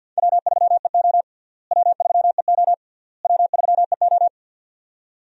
Morse code, heard as a keyed tone, about 51 wpm, 700 Hz. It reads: W3EO W3EO W3EO
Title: USA Call Signs - 4 Characters - Repeated 3x - 50wpm